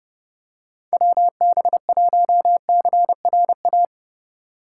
Morse code, heard as a keyed tone, about 30 wpm, 700 Hz. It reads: WB1CRA
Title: USA Call Signs - 6 Characters - 2x Character Spacing - 30wpm